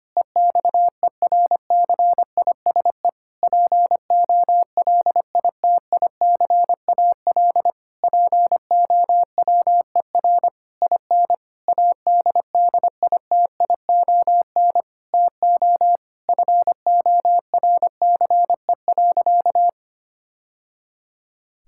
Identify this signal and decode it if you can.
Morse code, heard as a keyed tone, about 25 wpm, 700 Hz. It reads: EXERCISE POLITICAL POWER IN ADDITION TO FORCE.
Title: Sentences from Top 1000 Words - Review Part 6 - 25wpm